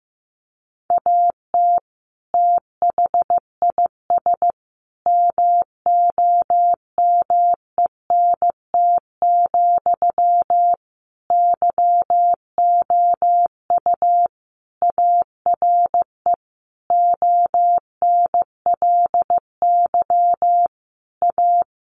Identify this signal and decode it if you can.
Morse code, heard as a keyed tone, about 15 wpm, 700 Hz. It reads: AT THIS MOMENT, YOU ARE ONLY A
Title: Sentences from Top 1000 Words 15wpm